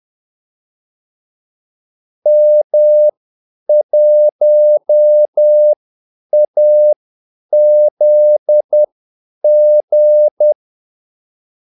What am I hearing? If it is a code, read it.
Morse code, heard as a keyed tone, about 10 wpm, spaced as for 8 wpm, 600 Hz. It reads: M1AZG
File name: Callsigns-10-8wpm